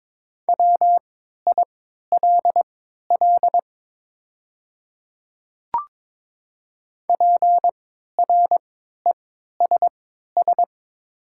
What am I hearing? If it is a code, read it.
Morse code, heard as a keyed tone, about 22 wpm, spaced as for 12 wpm, 700 Hz. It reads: WILL PRESS
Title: 3x Character Spacing - Top 500 Words - 22wpm